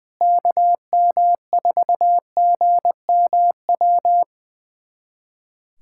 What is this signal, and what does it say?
Morse code, heard as a keyed tone, about 20 wpm, 700 Hz. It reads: KM4GMW
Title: USA Callsign DE Callsign - 6 Characters - 20wpm